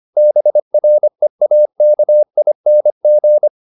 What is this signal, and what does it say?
Morse code, heard as a keyed tone, about 25 wpm, 600 Hz. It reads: BREAKING
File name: the old man and the sea part 1 25wpm